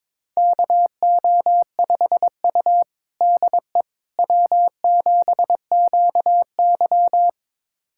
Morse code, heard as a keyed tone, about 22 wpm, 700 Hz. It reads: KO5U DE W7QY
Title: USA Callsign DE Callsign - 4 Characters - 22wpm